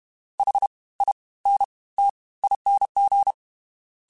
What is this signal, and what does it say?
Morse code, heard as a keyed tone, about 32 wpm, 800 Hz. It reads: H I N T ING
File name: Special INGs / Special INGs-mp3s / Fast ing @30